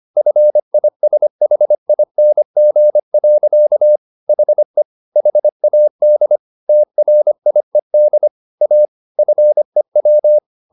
Morse code, heard as a keyed tone, about 25 wpm, 600 Hz. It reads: FISHING. HE HAD TRIED A FEW